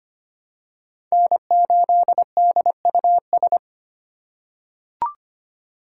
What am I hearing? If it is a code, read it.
Morse code, heard as a keyed tone, about 25 wpm, 700 Hz. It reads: N8DUS
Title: K1USN SST QSO Practice - 25wpm